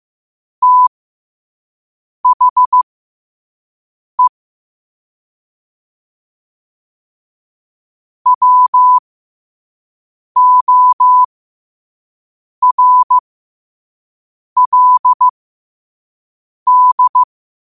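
Morse code, heard as a keyed tone, about 15 wpm, spaced as for 5 wpm, 1000 Hz. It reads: THE WORLD
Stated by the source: Morse code quote of the day sent at 5 wpm.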